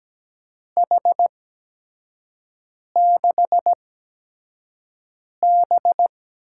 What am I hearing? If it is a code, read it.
Morse code, heard as a keyed tone, about 17 wpm, spaced as for 5 wpm, 700 Hz. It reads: H6B
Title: USA Call Signs - 4 Characters - 8x Character Spacing - 17wpm